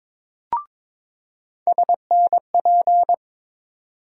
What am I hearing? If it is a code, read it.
Morse code, heard as a keyed tone, about 22 wpm, 700 Hz. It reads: SNP
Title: USA Call Sign Postfixes - 3 Letters 22wpm